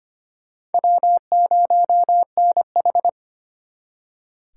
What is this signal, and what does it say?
Morse code, heard as a keyed tone, about 32 wpm, 700 Hz. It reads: W0NH